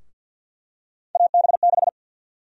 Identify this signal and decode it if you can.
Morse code, heard as a keyed tone, about 50 wpm, 700 Hz. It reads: AB6
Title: USA Call Sign Prefixes - 50wpm